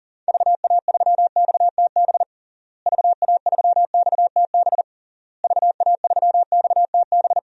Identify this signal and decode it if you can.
Morse code, heard as a keyed tone, about 40 wpm, 700 Hz. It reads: VA3XTB VA3XTB VA3XTB
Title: DXCC-100 Call Signs - 6 Characters - Repeated 3x - 40wpm